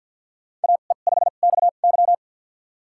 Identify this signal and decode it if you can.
Morse code, heard as a keyed tone, about 51 wpm, spaced as for 36 wpm, 700 Hz. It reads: AE5XY
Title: USA Call Signs - 5 Characters - 2x Character Spacing - 50wpm